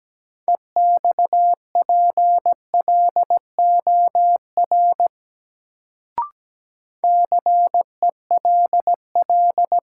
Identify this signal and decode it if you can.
Morse code, heard as a keyed tone, about 17 wpm, 700 Hz. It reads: EXPLOR CELL
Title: Sentences from Top 1100 Words - New Word Drill - 17wpm